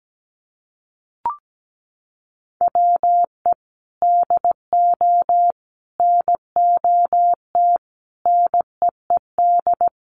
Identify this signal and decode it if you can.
Morse code, heard as a keyed tone, about 17 wpm, 700 Hz. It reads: WE DO NOT NEED
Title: Sets of 5 Words Encore 17wpm